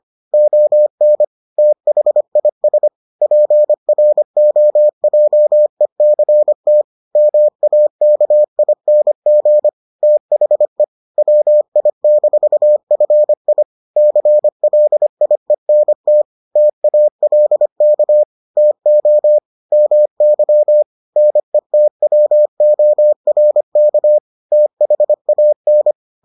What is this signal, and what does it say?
Morse code, heard as a keyed tone, about 25 wpm, 600 Hz. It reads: ON THIS PROJECT MAKING THE WI-FI CLIENT TALK TO MY NETWORK THAN